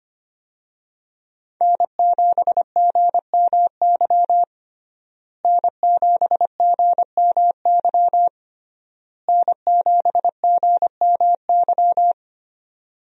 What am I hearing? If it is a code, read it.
Morse code, heard as a keyed tone, about 25 wpm, 700 Hz. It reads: N7GMY N7GMY N7GMY
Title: USA Call Signs - 5 Characters - Repeated 3x - 25wpm